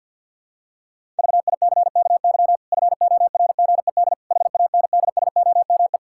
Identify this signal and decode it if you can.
Morse code, heard as a keyed tone, about 51 wpm, 700 Hz. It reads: VICKY FORGED HANDSOME
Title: Pangram Challenge v2 - 50wpm